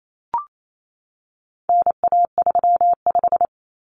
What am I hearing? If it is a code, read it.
Morse code, heard as a keyed tone, about 28 wpm, 700 Hz. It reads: NA35
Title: Common Cut Numbers - 4-Digit 28wpm